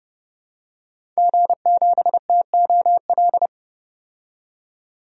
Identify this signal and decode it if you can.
Morse code, heard as a keyed tone, about 30 wpm, 700 Hz. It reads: G7TOL
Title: DXCC-100 Call Signs - 5 Characters - 30wpm